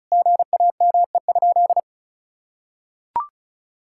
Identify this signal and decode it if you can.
Morse code, heard as a keyed tone, about 34 wpm, 700 Hz. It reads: GAME?
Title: Sentences from Top 1000 Words - Review Part 5 - 35wpm